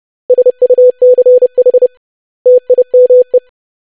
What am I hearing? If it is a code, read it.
Morse code, heard as a keyed tone, about 30 wpm, 500 Hz. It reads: SUCH TIME
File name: lesson / web / basic100-25wpm